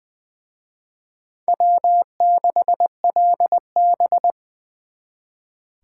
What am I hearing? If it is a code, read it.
Morse code, heard as a keyed tone, about 24 wpm, 700 Hz. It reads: W6LB
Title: Speed Racing - USA 1x2 Call Signs 20wpm